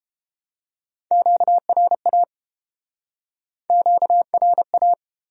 Morse code, heard as a keyed tone, about 33 wpm, 700 Hz. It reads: QRA QRA